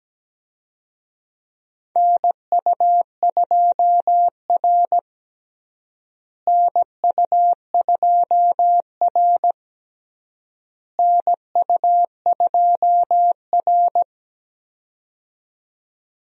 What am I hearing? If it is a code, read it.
Morse code, heard as a keyed tone, about 17 wpm, 700 Hz. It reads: NU2R NU2R NU2R
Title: USA Call Signs - 4 Characters - Repeated 3x - 17wpm